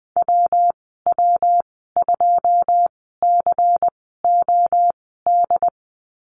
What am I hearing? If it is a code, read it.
Morse code, heard as a keyed tone, about 20 wpm, spaced as for 15 wpm, 700 Hz. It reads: WW2COD
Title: 219 Lesson 35g - Number 0 - Call Signs 20wpm